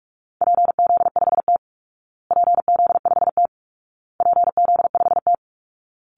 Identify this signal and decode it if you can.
Morse code, heard as a keyed tone, about 44 wpm, 700 Hz. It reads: PZ5T PZ5T PZ5T
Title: Call Signs Repeated 2x - 45wpm